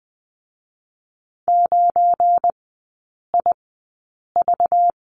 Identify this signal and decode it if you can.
Morse code, heard as a keyed tone, about 20 wpm, spaced as for 8 wpm, 700 Hz. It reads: 9IV
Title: 191 Lesson 31e - Letter v - Three Characters 20wpm